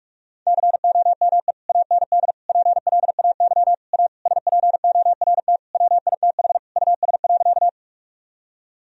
Morse code, heard as a keyed tone, about 46 wpm, 700 Hz. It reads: COME AND PLAY A SPORT WITH US.
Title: Sentences from Top 1000 Words - Review Part 4 - 45wpm